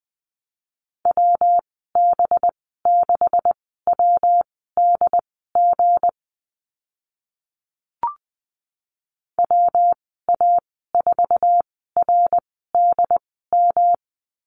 Morse code, heard as a keyed tone, about 20 wpm, spaced as for 15 wpm, 700 Hz. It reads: WB6WDG WA4RDM
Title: 186 Lesson 30g - Letter b - Call Signs 20wpm